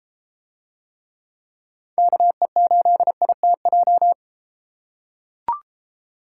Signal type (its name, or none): Morse code